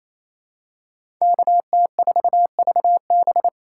Morse code, heard as a keyed tone, about 28 wpm, 700 Hz. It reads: KT4VB
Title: USA Call Signs - 3x Character Spacing - 28wpm